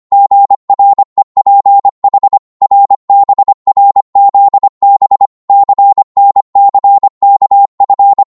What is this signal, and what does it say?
Morse code, heard as a keyed tone, about 25 wpm, 800 Hz. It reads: GREPH RBRZB CNCKF